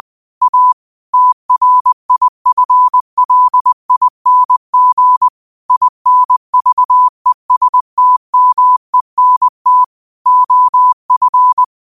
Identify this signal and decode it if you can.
Morse code, heard as a keyed tone, about 20 wpm, 1000 Hz. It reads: A TRIFLING INVESTMENT OF
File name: qotdfiles / 20wpm